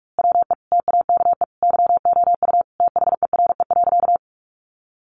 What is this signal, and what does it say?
Morse code, heard as a keyed tone, about 44 wpm, 700 Hz. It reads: WE TAKE YOU THERE.
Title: A Sound Of Thunder 45wpm